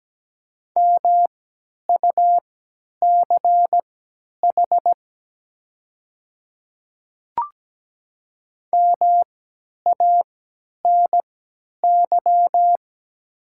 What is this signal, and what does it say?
Morse code, heard as a keyed tone, about 17 wpm, spaced as for 10 wpm, 700 Hz. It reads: MUCH MANY
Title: 3x Character Spacing - Top 500 Words - 17wpm